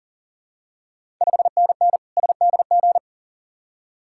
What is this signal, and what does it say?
Morse code, heard as a keyed tone, about 42 wpm, 700 Hz. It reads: 5NN SDG
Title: Ontario QSO Party Exchange 40wpm